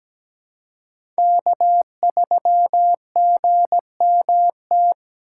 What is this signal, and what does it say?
Morse code, heard as a keyed tone, about 17 wpm, 700 Hz. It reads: K3GMT
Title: K1USN SST QSO Practice - 17wpm